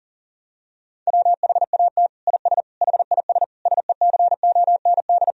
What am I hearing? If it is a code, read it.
Morse code, heard as a keyed tone, about 40 wpm, 700 Hz. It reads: WHAT IS HIS SECOND
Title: Sentences from Top 1000 Words - Review Part 1 - 40wpm